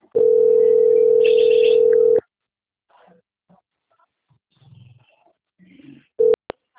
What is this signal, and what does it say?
Ringback tone (North America), 440 + 480 Hz.
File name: out- - - - - .18751